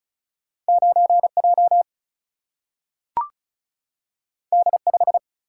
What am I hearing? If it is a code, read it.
Morse code, heard as a keyed tone, about 36 wpm, 700 Hz. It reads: 9 J D 5
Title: Single-Letter Single-Number 35wpm